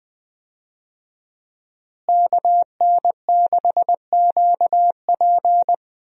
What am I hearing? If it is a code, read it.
Morse code, heard as a keyed tone, about 19 wpm, 700 Hz. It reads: KN6QP
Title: USA Call Signs - 5 Characters - Repeated 3x - 20wpm